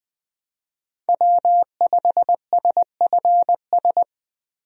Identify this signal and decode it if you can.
Morse code, heard as a keyed tone, about 20 wpm, 700 Hz. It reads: W5SFS
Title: USA Call Signs - 5 Characters - 20wpm